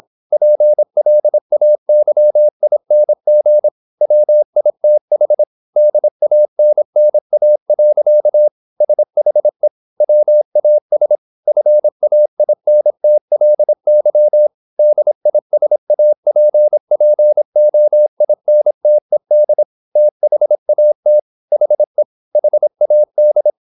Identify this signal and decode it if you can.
Morse code, heard as a keyed tone, about 26 wpm, 600 Hz. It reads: PLAYING WITH DANNA. SHE WAS FAINTLY DISAPPOINTED THAT HE HAD
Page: Never 26wpm | CW med Gnister